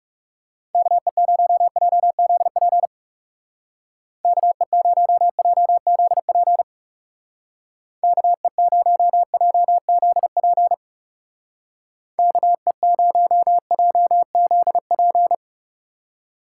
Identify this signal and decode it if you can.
Morse code, heard as a keyed tone, about 36 wpm, 700 Hz. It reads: KE0JZP KE0JZP KE0JZP KE0JZP
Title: Speed Racing - USA 2x3 Call Signs 30wpm